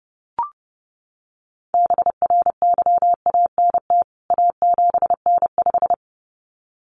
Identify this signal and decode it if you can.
Morse code, heard as a keyed tone, about 30 wpm, 700 Hz. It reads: BRYANT A7N5
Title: CWT Exchange 30wpm